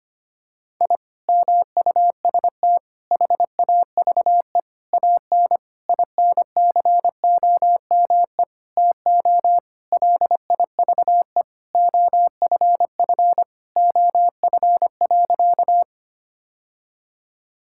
Morse code, heard as a keyed tone, about 25 wpm, 700 Hz. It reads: I MUST HAVE AN INCOME TO LIVE OFF OF.
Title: Sentences from Top 1000 Words - Review Part 5 - 25wpm